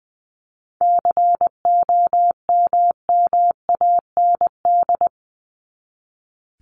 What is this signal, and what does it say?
Morse code, heard as a keyed tone, about 20 wpm, 700 Hz. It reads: COMMAND
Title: Top 1501-1750 Words 20wpm